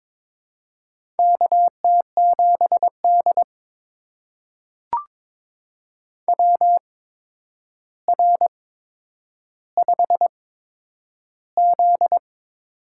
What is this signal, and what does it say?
Morse code, heard as a keyed tone, about 22 wpm, spaced as for 9 wpm, 700 Hz. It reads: KT7D WR5Z
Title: USA Call Signs - 4 Characters - 8x Character Spacing - 22wpm